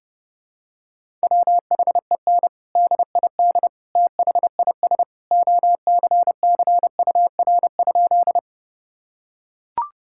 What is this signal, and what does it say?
Morse code, heard as a keyed tone, about 30 wpm, 700 Hz. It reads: WHEN DID THIS OCCUR?
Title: Sentences from Top 1000 Words - Review Part 6 - 30wpm